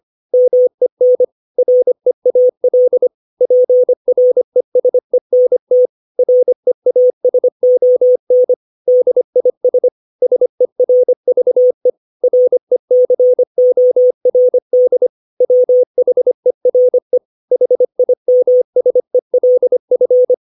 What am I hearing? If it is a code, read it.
Morse code, heard as a keyed tone, about 25 wpm, 500 Hz. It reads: MEN REAL PRESENT REASON DIS SERVE RECORD WHERE HIMSELF
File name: 632 words / 632 Words 25WPM